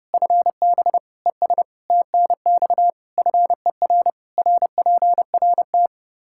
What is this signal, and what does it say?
Morse code, heard as a keyed tone, about 32 wpm, 700 Hz. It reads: FB ES TNX FER RPRT
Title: Practice QSO - Exchange 2 - Info - 30wpm